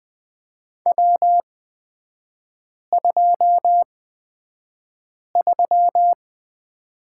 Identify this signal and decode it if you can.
Morse code, heard as a keyed tone, about 20 wpm, spaced as for 5 wpm, 690 Hz. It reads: W23
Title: 229 Lesson 37e - Letter x - Three Characters 20wpm